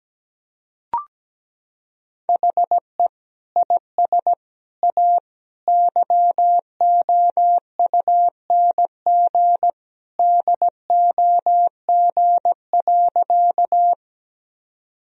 Morse code, heard as a keyed tone, about 17 wpm, 700 Hz. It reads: HE IS A YOUNG DOG.
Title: Sentences from Top 1000 Words - Review Part 2 - 17wpm